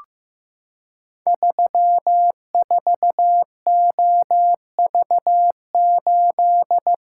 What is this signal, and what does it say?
Morse code, heard as a keyed tone, about 15 wpm, 700 Hz. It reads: 34OV8
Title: Code Groups - 5 Letter:Numbers - 15wpm